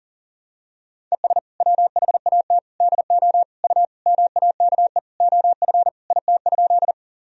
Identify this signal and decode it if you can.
Morse code, heard as a keyed tone, about 40 wpm, 700 Hz. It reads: ES WHAT DO U MAKE OF IT?
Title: Sentences from Top 100 Words and Abbreviations 40wpm